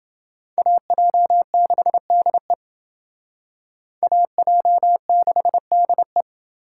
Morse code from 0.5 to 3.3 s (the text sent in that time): AJ6DE